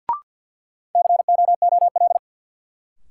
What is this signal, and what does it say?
Morse code, heard as a keyed tone, about 52 wpm, 700 Hz. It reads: COOL
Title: Sentences from Top 1200 Words - New Word Drill Alt - 50wpm